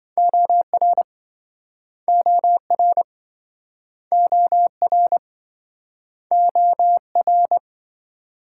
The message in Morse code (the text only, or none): OR OR OR OR